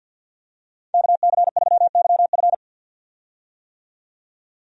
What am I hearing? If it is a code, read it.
Morse code, heard as a keyed tone, about 49 wpm, 700 Hz. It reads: KC3YF
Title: USA Call Signs - 5 Characters - 50wpm